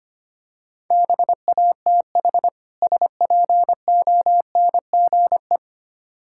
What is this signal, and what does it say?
Morse code, heard as a keyed tone, about 25 wpm, 700 Hz. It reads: BATH SPONGE
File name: practice / 25wpm / Difficult Words - Speed-Racing with Hints - 25wpm